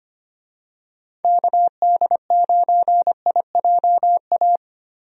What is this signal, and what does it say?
Morse code, heard as a keyed tone, about 25 wpm, 700 Hz. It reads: KD9IJA
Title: USA Call Signs - 6 Characters - Repeated 3x - 25wpm